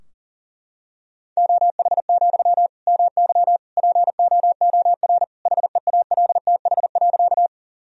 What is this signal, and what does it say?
Morse code, heard as a keyed tone, about 40 wpm, 700 Hz. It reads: OH, MY POOR HEALTH.
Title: Sentences from Top 1000 Words - Review Part 3 - 40wpm